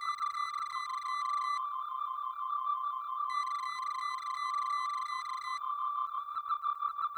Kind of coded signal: none